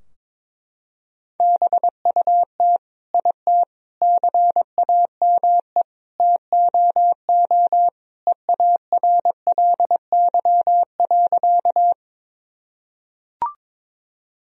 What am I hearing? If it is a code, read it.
Morse code, heard as a keyed tone, about 22 wpm, 700 Hz. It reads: BUT IT CAME TOO EARLY.
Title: Sentences from Top 1000 Words - Review Part 2 - 22wpm